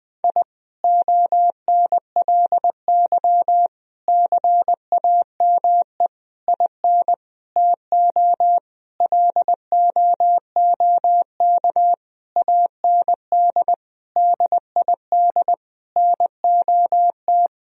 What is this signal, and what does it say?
Morse code, heard as a keyed tone, about 21 wpm, 700 Hz. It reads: I ONLY CAME IN TO LOOK AND DID NOT
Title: Sentences from Top 1000 Words - Review Part 2 - 20wpm